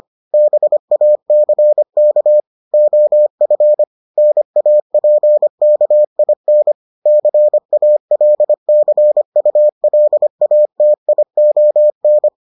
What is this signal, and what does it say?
Morse code, heard as a keyed tone, about 25 wpm, 600 Hz. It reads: BACK OF NAPKIN CALCULATION